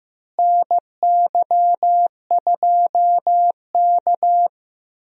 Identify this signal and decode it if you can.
Morse code, heard as a keyed tone, about 15 wpm, 700 Hz. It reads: NY2K
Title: USA Call Signs - 4 Characters - Repeated 3x - 15wpm